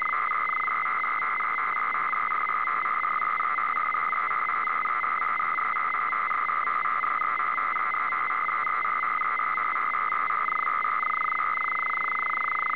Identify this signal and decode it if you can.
Two-tone teleprinter signal, about 850 Hz shift.
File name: D-AF300 Bd